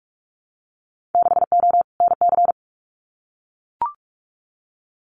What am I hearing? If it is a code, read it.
Morse code, heard as a keyed tone, about 44 wpm, 700 Hz. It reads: 6O NC
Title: Winter Field Day Exchanges 45wpm